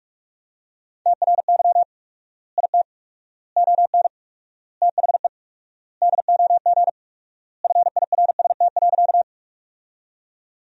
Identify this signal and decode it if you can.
Morse code, heard as a keyed tone, about 46 wpm, 700 Hz. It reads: TRY IT ON THE DOG FIRST.
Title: 4x Word-Spacing - Sentences from Top 400 Words - 45wpm